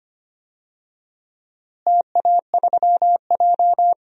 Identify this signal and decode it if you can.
Morse code, heard as a keyed tone, about 25 wpm, 700 Hz. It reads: TA3J
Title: DXCC-100 Call Signs - 4 Characters - Repeated 3x - 25wpm